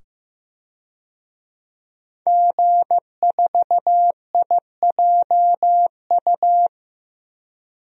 Morse code, heard as a keyed tone, about 15 wpm, 700 Hz. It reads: G4IJU
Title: DXCC-100 Call Signs - 5 Characters - 15wpm